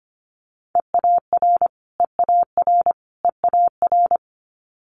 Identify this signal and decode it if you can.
Morse code, heard as a keyed tone, about 25 wpm, 700 Hz. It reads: EAR EAR EAR
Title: Multi-letter Phonemes Repeated 3x 25wpm